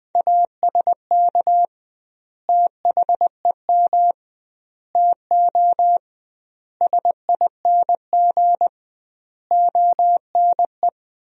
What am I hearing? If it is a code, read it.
Morse code, heard as a keyed tone, about 20 wpm, 700 Hz. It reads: ASK THEM TO SING ONE
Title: 2x Word-Spacing - Sentences from Top 400 Words - 20wpm